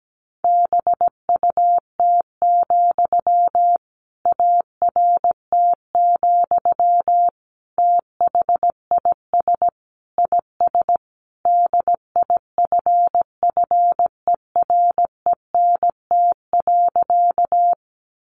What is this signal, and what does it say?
Morse code, heard as a keyed tone, about 17 wpm, 700 Hz. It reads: BUT, ART, THIS IS DIFFERENT.
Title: Sentences from Top 600 Words 17wpm